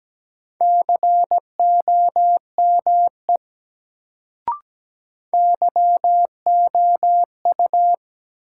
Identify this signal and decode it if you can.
Morse code, heard as a keyed tone, about 17 wpm, 700 Hz. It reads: COME YOU
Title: Sentences from Top 100 Words - New Word Drill - 17wpm